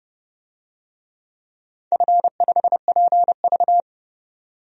Morse code, heard as a keyed tone, about 30 wpm, 700 Hz. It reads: F5PV
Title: DXCC-100 Call Signs - 4 Characters - 30wpm